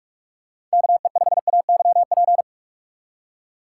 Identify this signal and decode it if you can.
Morse code, heard as a keyed tone, about 43 wpm, 700 Hz. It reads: KE5AYP
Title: USA Call Signs - 4x Character Spacing - 45wpm